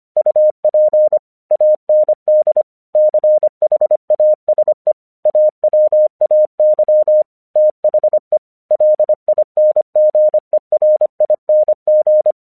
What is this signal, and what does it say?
Morse code, heard as a keyed tone, about 25 wpm, 610 Hz. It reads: UP AND CHASE AWAY THE LINGERING
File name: 1 English 25wpm / AQC15